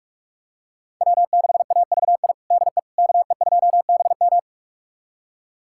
Morse code, heard as a keyed tone, about 46 wpm, 700 Hz. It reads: W6AVI DE KE2BM
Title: USA Callsign DE Callsign - 5 Characters - 45wpm